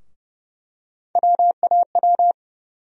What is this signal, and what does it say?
Morse code, heard as a keyed tone, about 29 wpm, 700 Hz. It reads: WAW
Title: USA Call Sign Postfixes - 3 Letters 30wpm